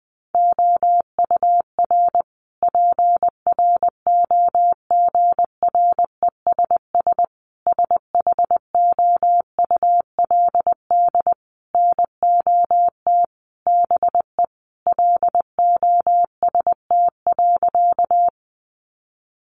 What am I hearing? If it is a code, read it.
Morse code, heard as a keyed tone, about 20 wpm, 700 Hz. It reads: OUR PROGRESS SHOULD NOT BE LOST.
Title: Sentences from Top 1400 Words 20wpm